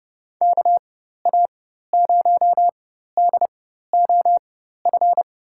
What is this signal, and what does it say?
Morse code, heard as a keyed tone, about 31 wpm, spaced as for 14 wpm, 700 Hz. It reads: KA0DOF
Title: USA Call Signs - 6 Characters - 4x Character Spacing - 30wpm